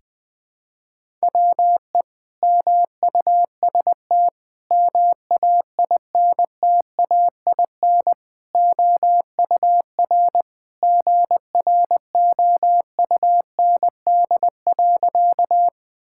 Morse code, heard as a keyed tone, about 19 wpm, 700 Hz. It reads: WE MUST MAINTAIN OUR GROUND.
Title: Sentences from Top 1000 Words - Review Part 5 - 20wpm